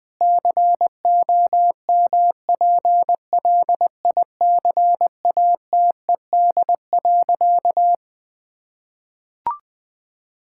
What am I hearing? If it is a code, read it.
Morse code, heard as a keyed tone, about 20 wpm, 700 Hz. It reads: COMPLICATED.
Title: Sentences with 6-Word Phrases - 20wpm